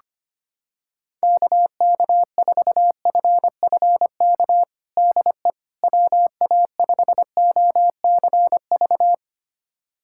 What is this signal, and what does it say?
Morse code, heard as a keyed tone, about 25 wpm, 700 Hz. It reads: KK4FFK DE WA5OCV
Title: USA Callsign DE Callsign - 6 Characters - 25wpm